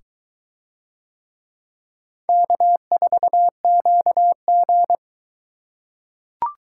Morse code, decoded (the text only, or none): K4QG